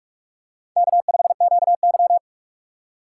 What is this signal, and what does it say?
Morse code, heard as a keyed tone, about 46 wpm, 700 Hz. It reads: K5QY
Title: USA Call Signs - 4 Characters - 6x Character Spacing - 45wpm